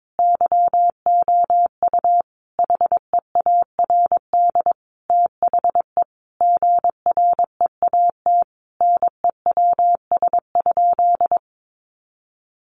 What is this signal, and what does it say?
Morse code, heard as a keyed tone, about 22 wpm, 700 Hz. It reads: YOU HEARD THE GREAT NEWS?
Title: Sentences from Top 800 Words 22wpm